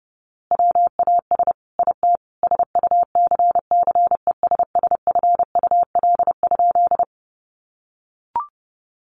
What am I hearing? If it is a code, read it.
Morse code, heard as a keyed tone, about 30 wpm, 700 Hz. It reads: WAS IT SUCCESSFUL?
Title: A Sound Of Thunder 30wpm